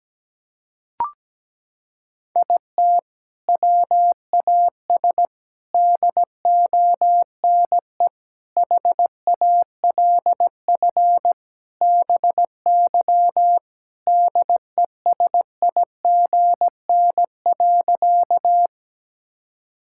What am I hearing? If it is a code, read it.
Morse code, heard as a keyed tone, about 17 wpm, 700 Hz. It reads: IT WAS DONE HALF BY DESIGN.
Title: Sentences from Top 1000 Words - Review Part 5 - 17wpm